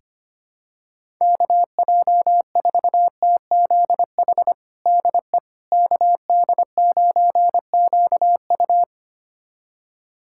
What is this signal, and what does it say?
Morse code, heard as a keyed tone, about 25 wpm, 700 Hz. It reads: KJ4TZH DE KD9QU
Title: USA Callsign DE Callsign 25wpm